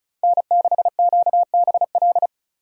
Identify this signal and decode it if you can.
Morse code, heard as a keyed tone, about 43 wpm, 700 Hz. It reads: N6QBL
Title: Speed Racing - USA 1x3 Call Signs 35wpm